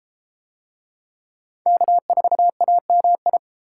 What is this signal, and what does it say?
Morse code, heard as a keyed tone, about 31 wpm, 700 Hz. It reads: K4AMI